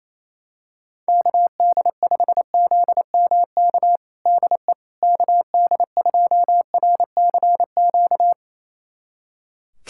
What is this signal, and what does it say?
Morse code, heard as a keyed tone, about 28 wpm, 700 Hz. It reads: KD5ZMK DE KD2RCQ
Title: USA Callsign DE Callsign - 6 Characters - 28wpm